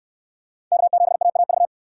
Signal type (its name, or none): Morse code